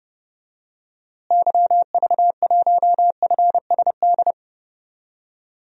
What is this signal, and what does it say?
Morse code, heard as a keyed tone, about 30 wpm, 700 Hz. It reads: YV1FSD
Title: DXCC-100 Call Signs - 6 Characters - Repeated 2x - 30wpm